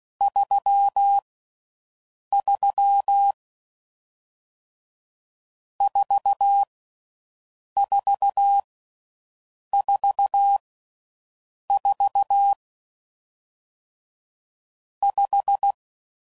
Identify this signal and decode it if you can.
Morse code, heard as a keyed tone, about 16 wpm, 800 Hz. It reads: 3 3 4 4 4 4 5